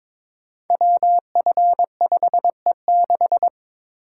Morse code, heard as a keyed tone, about 22 wpm, 700 Hz. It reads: WF5E6
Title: Code Groups - 5 Letter:Numbers - 22wpm